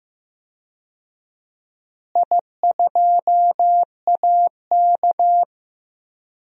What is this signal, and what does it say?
Morse code, heard as a keyed tone, about 15 wpm, 700 Hz. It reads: I2AK
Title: DXCC-100 Call Signs - 4 Characters - Repeated 3x - 15wpm